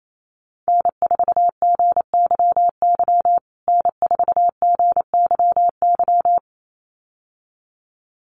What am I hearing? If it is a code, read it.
Morse code, heard as a keyed tone, about 28 wpm, 700 Hz. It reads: N4GYY N4GYY
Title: USA Call Signs Repeated 2x 28wpm